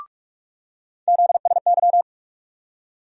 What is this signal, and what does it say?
Morse code, heard as a keyed tone, about 43 wpm, 700 Hz. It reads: ZSY
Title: USA Call Sign Postfixes - 3 Letters 45wpm